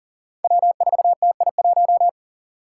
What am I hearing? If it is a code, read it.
Morse code, heard as a keyed tone, about 40 wpm, 700 Hz. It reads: W4TI1
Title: Five Challenging Letter-Number Combinations 40wpm